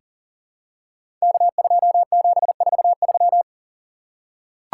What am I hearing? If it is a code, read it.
Morse code, heard as a keyed tone, about 40 wpm, 700 Hz. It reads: K2743
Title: Code Groups - 5 Letter:Numbers - 40wpm